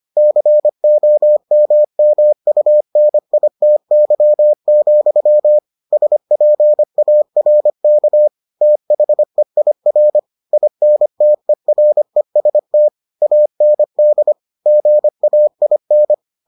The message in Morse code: COMMUNITY, SPARK THEIR INTEREST AND GAIN

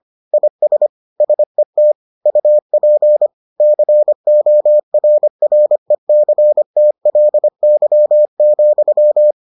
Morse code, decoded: IS SET UP CORRECTLY,

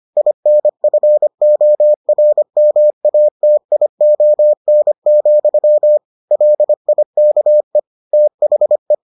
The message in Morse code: INFORMATION, LIKE THE

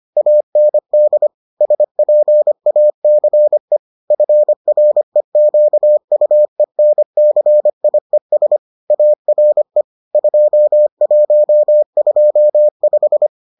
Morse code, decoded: AND SPACE FREQUENCIES ARE 2125

